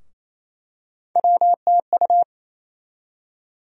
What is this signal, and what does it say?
Morse code, heard as a keyed tone, about 27 wpm, 700 Hz. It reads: WTU
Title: USA Call Sign Postfixes - 3 Letters 28wpm